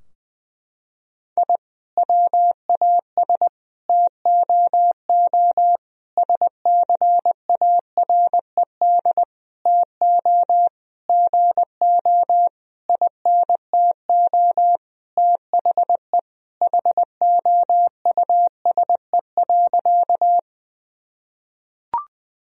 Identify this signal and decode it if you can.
Morse code, heard as a keyed tone, about 20 wpm, 700 Hz. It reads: I WAS TOO SCARED TO GO INTO THE HOUSE.
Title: Sentences from Top 1000 Words - Review Part 4 - 20wpm